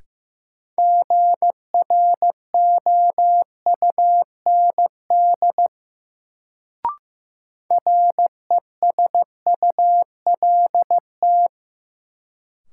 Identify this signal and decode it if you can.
Morse code, heard as a keyed tone, about 15 wpm, 700 Hz. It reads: GROUND RESULT
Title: Sets of 3 Words - New Word Drill - 15wpm